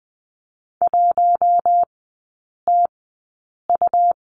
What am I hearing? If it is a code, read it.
Morse code, heard as a keyed tone, about 19 wpm, spaced as for 8 wpm, 700 Hz. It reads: 1TU
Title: 138 Lesson 23e - Letter f - Three Characters 20wpm